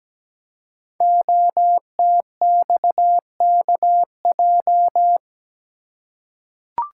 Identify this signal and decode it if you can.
Morse code, heard as a keyed tone, about 17 wpm, 700 Hz. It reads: OTXKJ
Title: Code Groups - 5 Letter:Numbers - 17wpm